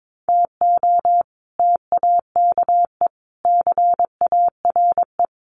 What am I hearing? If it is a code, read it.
Morse code, heard as a keyed tone, about 22 wpm, 700 Hz. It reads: TO TAKE CARE
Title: Sets of 5 Words Encore 22wpm